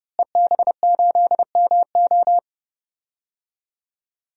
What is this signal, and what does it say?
Morse code, heard as a keyed tone, about 30 wpm, 700 Hz. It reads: EB8MO
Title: DXCC-100 Call Signs - 5 Characters - Repeated 2x - 30wpm